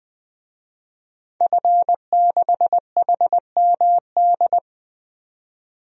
Morse code, heard as a keyed tone, about 20 wpm, 700 Hz. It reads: F6HMD
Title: DXCC-100 Call Signs - 5 Characters - 20wpm